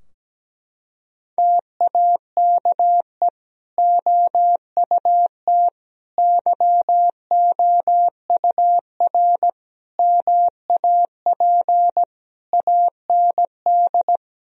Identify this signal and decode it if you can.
Morse code, heard as a keyed tone, about 17 wpm, 700 Hz. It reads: TAKE OUT YOUR MAP AND
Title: Sentences from Top 1000 Words - Review Part 2 - 17wpm